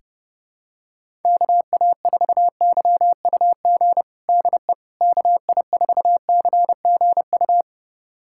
Morse code, heard as a keyed tone, about 30 wpm, 700 Hz. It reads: KA4YUG DE KI4CGU
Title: USA Callsign DE Callsign - 6 Characters - 30wpm